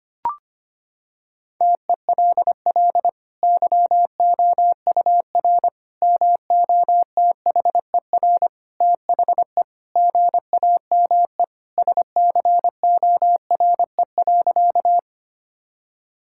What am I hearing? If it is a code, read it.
Morse code, heard as a keyed tone, about 25 wpm, 700 Hz. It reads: TELL YOUR MOTHER THE GAME SCORE.
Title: Sentences from Top 1600 Words 25wpm